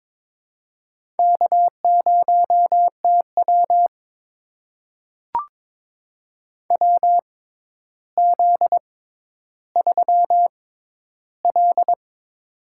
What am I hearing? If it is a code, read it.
Morse code, heard as a keyed tone, about 22 wpm, spaced as for 11 wpm, 700 Hz. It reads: K0TW WZ3L
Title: USA Call Signs - 4 Characters - 6x Character Spacing - 22wpm